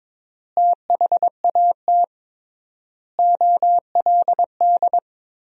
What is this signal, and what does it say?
Morse code, heard as a keyed tone, about 22 wpm, 700 Hz. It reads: THAT OLD
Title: 3x Word-Spacing - Sentences from Top 400 Words - 22wpm